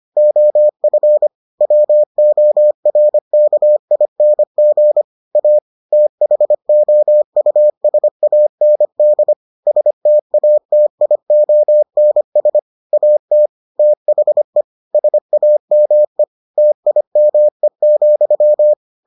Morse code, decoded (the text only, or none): OF WORKING A THOUSAND STATIONS AT THE SAME TIME,